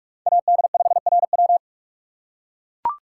Morse code, heard as a keyed tone, about 44 wpm, 700 Hz. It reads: AD5RW
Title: USA Call Signs - 5 Characters - 6x Character Spacing - 45wpm